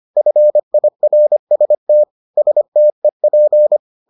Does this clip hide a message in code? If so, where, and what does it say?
Yes: Morse code from 0.2 to 3.8 s, heard as a keyed tone, about 25 wpm, 600 Hz. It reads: FIRST STEP